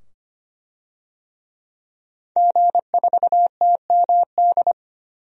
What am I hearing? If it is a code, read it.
Morse code, heard as a keyed tone, about 25 wpm, 700 Hz. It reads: G4TMD
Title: DXCC-100 Call Signs - 5 Characters - 25wpm